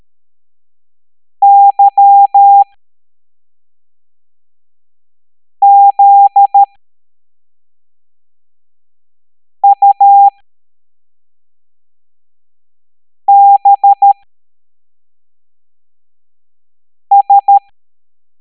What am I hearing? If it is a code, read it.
Morse code, heard as a keyed tone, about 13 wpm, spaced as for 3 wpm, 800 Hz. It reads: YZUBS
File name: hamdata / L05-EX01-SVY